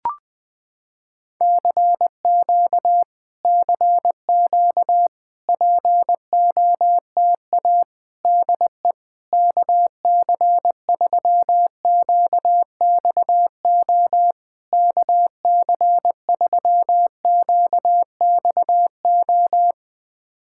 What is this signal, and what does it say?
Morse code, heard as a keyed tone, about 20 wpm, 700 Hz. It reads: CQ CQ POTA DE KC3QXO KC3QXO
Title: POTA QSO Practice - 20wpm